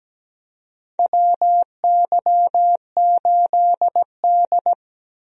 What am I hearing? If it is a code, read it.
Morse code, heard as a keyed tone, about 17 wpm, 700 Hz. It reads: WY8D
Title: USA Call Signs - 4 Characters - Repeated 3x - 17wpm